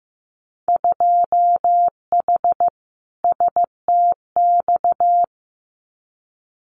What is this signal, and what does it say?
Morse code, heard as a keyed tone, about 15 wpm, 700 Hz. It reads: 2H STX
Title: Winter Field Day Exchanges Encore 15wpm